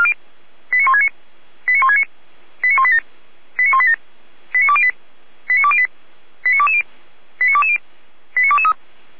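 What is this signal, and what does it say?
Five-tone selective call (ZVEI), heard as quick runs of five tones, tones of about 70 ms. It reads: 89169 89169 89187 89187 89208 89208 89220 89220 89223
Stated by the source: ZVEI five-tone call received by SDR.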